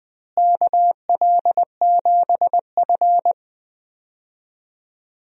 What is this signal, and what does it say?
Morse code, heard as a keyed tone, about 20 wpm, 700 Hz. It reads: KL7F
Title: DXCC-100 Call Signs - 4 Characters - Repeated 3x - 20wpm